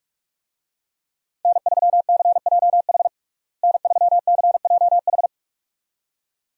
Morse code, heard as a keyed tone, about 46 wpm, 700 Hz. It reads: N3CJH N3CJH
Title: USA Call Signs - 5 Characters - Repeated 2x - 45wpm